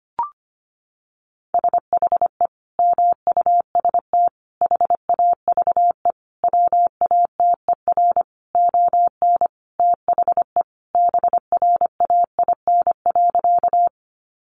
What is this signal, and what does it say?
Morse code, heard as a keyed tone, about 25 wpm, 700 Hz. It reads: SHE MUST HAVE WATER ON THE BRAIN.
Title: Sentences from Top 1100 Words 25wpm